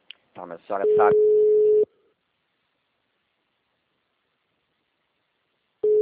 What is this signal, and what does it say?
Ringback tone (Europe, 425 Hz), with speech.